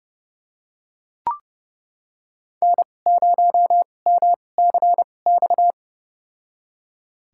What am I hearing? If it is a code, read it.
Morse code, heard as a keyed tone, about 30 wpm, spaced as for 22 wpm, 700 Hz. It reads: N0MCX
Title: USA Call Signs - 5 Characters - 2x Character Spacing - 30wpm